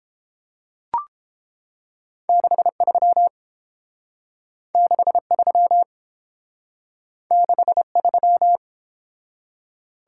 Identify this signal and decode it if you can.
Morse code, heard as a keyed tone, about 29 wpm, 700 Hz. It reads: 63 63 63